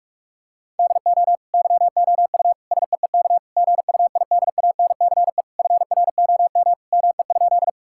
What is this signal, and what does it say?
Morse code, heard as a keyed tone, about 46 wpm, 700 Hz. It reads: DO YOU SEEK GUIDANCE FROM ME?
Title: Sentences from Top 1600 Words 45wpm